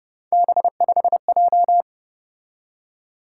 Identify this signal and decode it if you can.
Morse code, heard as a keyed tone, about 29 wpm, 700 Hz. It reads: B5J